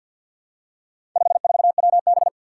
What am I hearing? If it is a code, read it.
Morse code, heard as a keyed tone, about 53 wpm, 700 Hz. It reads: 54WB
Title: Four Challenging Letter-Number Combinations 50wpm